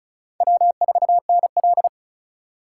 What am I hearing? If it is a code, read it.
Morse code, heard as a keyed tone, about 34 wpm, 700 Hz. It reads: W4NL
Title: USA Call Signs - 4 Characters - 3x Character Spacing - 35wpm